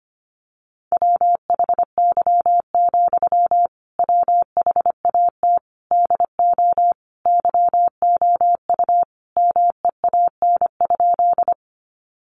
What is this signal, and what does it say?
Morse code, heard as a keyed tone, about 25 wpm, 700 Hz. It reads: WHY, WHAT DO YOU MEAN?
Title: Sentences from Top 200 Words 25wpm